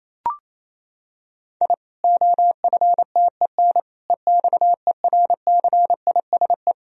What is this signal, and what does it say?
Morse code, heard as a keyed tone, about 28 wpm, 700 Hz. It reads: I OFTEN EXERCISE
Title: Sentences from Top 1000 Words - Review Part 4 - 28wpm